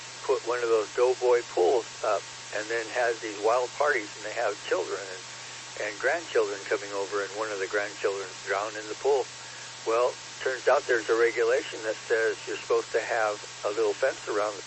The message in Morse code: none

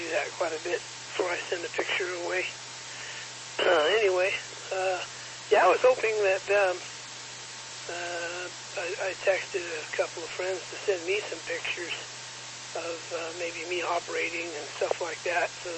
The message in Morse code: none

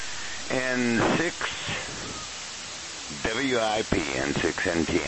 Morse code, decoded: none

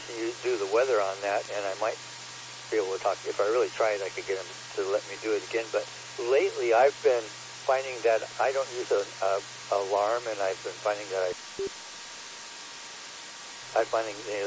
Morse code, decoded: none

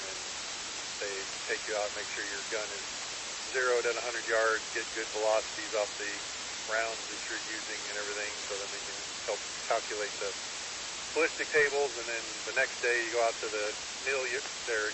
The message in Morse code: none

